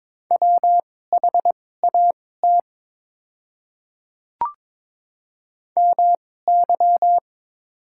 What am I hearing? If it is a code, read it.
Morse code, heard as a keyed tone, about 22 wpm, spaced as for 16 wpm, 700 Hz. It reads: WHAT MY
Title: 2x Character Spacing - Top 100 Words - 22wpm